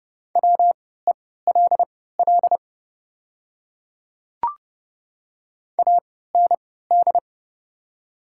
Morse code, heard as a keyed tone, about 30 wpm, spaced as for 17 wpm, 700 Hz. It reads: WELL AND
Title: 3x Character Spacing - Top 300 Words - 30wpm